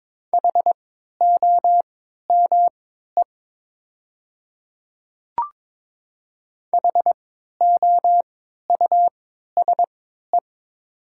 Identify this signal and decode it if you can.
Morse code, heard as a keyed tone, about 22 wpm, spaced as for 12 wpm, 700 Hz. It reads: HOME HOUSE
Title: 3x Character Spacing - Top 200 Words - 22wpm